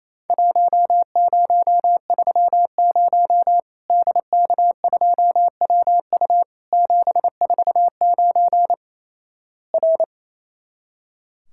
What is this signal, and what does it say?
Morse code, heard as a keyed tone, about 28 wpm, 700 Hz. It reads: 1030 DK2WU 749 R
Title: Worked All Europe DX Contest - QTC Exchange - 28wpm